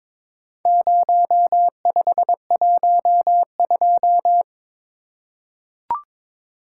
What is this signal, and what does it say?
Morse code, heard as a keyed tone, about 22 wpm, 700 Hz. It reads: 0512
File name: practice / 22wpm / Four Letter-Number Combinations - 22wpm